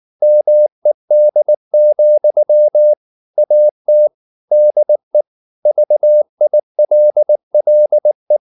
Morse code, heard as a keyed tone, about 19 wpm, 600 Hz. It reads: MED, AT DE VILLE
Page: Ferie 19 wpm